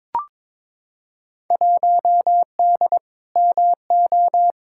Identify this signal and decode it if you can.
Morse code, heard as a keyed tone, about 22 wpm, 700 Hz. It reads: 1D MO
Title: Field Day Exchanges - 22wpm